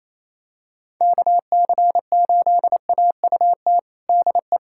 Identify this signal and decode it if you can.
Morse code, heard as a keyed tone, about 28 wpm, 700 Hz. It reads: KC8AUT DE
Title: USA Callsign DE Callsign - 6 Characters - 28wpm